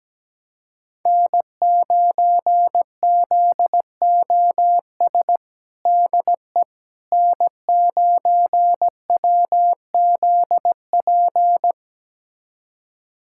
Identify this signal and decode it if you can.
Morse code, heard as a keyed tone, about 17 wpm, 700 Hz. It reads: N9ZOS DE N9WZP
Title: USA Callsign DE Callsign - 5 Characters - 17wpm